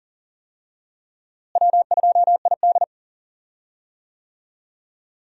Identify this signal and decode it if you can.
Morse code, heard as a keyed tone, about 40 wpm, 700 Hz. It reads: W2ID
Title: USA Call Signs - 4 Characters - Repeated 3x - 40wpm